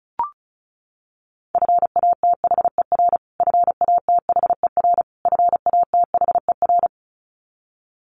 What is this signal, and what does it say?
Morse code, heard as a keyed tone, about 34 wpm, 700 Hz. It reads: FATHER FATHER FATHER
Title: Top 200 Words Repeated 35wpm